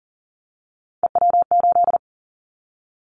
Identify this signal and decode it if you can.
Morse code, heard as a keyed tone, about 40 wpm, 700 Hz. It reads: EW8
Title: Callsign Prefixes for Top 100 DXCC - 40wpm